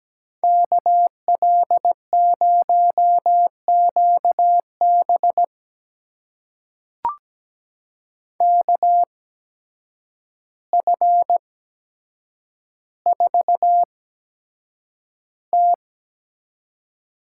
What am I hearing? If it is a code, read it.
Morse code, heard as a keyed tone, about 17 wpm, spaced as for 8 wpm, 700 Hz. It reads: KL0QB KF4T
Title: USA Call Signs - 5 Characters - 8x Character Spacing - 17wpm